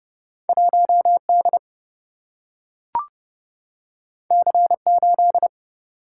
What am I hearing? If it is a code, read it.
Morse code, heard as a keyed tone, about 30 wpm, 700 Hz. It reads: 1 D C 8
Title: Single-Letter Single-Number 30wpm